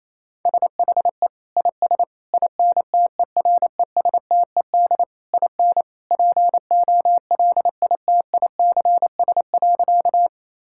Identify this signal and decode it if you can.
Morse code, heard as a keyed tone, about 28 wpm, 700 Hz. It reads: SHE IS INTERESTED IN POLITICS.
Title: Sentences from Top 1000 Words - Review Part 6 - 28wpm